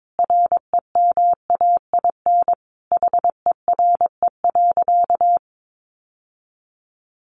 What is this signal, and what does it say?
Morse code, heard as a keyed tone, about 22 wpm, 700 Hz. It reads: REMAIN HERE.
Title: Sentences from Top 1200 Words 22wpm